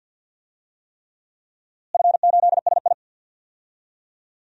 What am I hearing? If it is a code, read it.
Morse code, heard as a keyed tone, about 52 wpm, 700 Hz. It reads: F8SI
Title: DXCC-100 Call Signs - 4 Characters - Repeated 3x - 50wpm